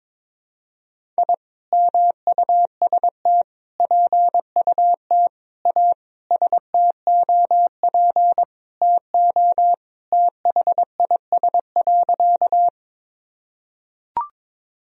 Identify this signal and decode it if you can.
Morse code, heard as a keyed tone, about 22 wpm, 700 Hz. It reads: I MUST PUT A STOP TO THIS.
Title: Sentences from Top 1000 Words - Review Part 1 - 22wpm